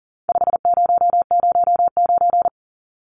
Morse code, heard as a keyed tone, about 40 wpm, 700 Hz. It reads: 5009
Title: Numbers - 4-Digit 40wpm